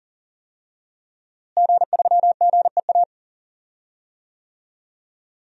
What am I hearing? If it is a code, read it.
Morse code, heard as a keyed tone, about 40 wpm, 700 Hz. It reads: G3GEA
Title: DXCC-100 Call Signs - 5 Characters - Repeated 3x - 40wpm